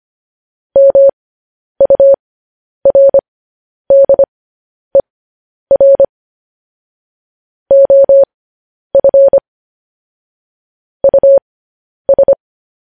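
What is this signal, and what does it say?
Morse code, heard as a keyed tone, about 25 wpm, spaced as for 10 wpm, 550 Hz. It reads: MURDER OF US
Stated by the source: Morse code news headlines sent at 10 wpm.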